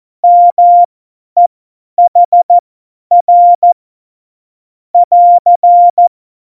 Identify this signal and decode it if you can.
Morse code, heard as a keyed tone, about 14 wpm, spaced as for 10 wpm, 700 Hz. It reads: MEHR <AR>
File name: Morsekurs / Witze / Witz 21. November 2017, 50 F70